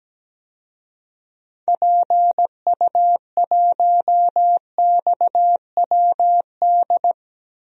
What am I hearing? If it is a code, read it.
Morse code, heard as a keyed tone, about 17 wpm, 700 Hz. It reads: PU1XWD
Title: DXCC-100 Call Signs - 6 Characters - Repeated 2x - 17wpm